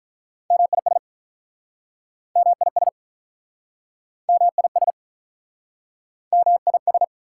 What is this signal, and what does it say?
Morse code, heard as a keyed tone, about 46 wpm, 690 Hz. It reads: MIS MIS MIS MIS